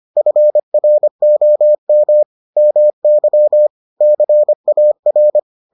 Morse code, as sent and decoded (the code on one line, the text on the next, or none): ..-. .-. --- -- / -- -.-- / -.-. .- .-.
FROM MY CAR